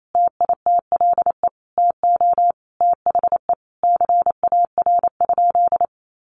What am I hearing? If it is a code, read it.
Morse code, heard as a keyed tone, about 28 wpm, 700 Hz. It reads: TITLE TO THE CAR?
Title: Sentences from Top 1200 Words 28wpm